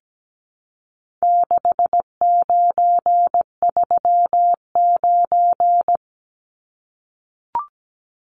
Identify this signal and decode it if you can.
Morse code, heard as a keyed tone, about 17 wpm, 700 Hz. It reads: 6939
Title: Numbers - 4-Digit 17wpm